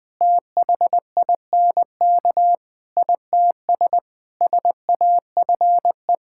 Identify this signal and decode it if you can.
Morse code, heard as a keyed tone, about 20 wpm, 700 Hz. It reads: THINK ITS SAFE
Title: Sets of 6 Words - 20wpm